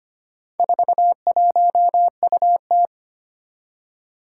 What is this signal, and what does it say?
Morse code, heard as a keyed tone, about 25 wpm, 700 Hz. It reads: 41UT
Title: Four Challenging Letter-Number Combinations 25wpm